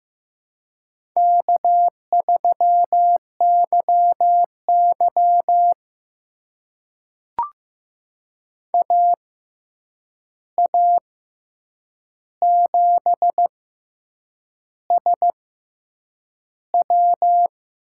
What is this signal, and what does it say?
Morse code, heard as a keyed tone, about 15 wpm, spaced as for 7 wpm, 700 Hz. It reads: K3YY AA7SW
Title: USA Call Signs - 6x Character Spacing - 15wpm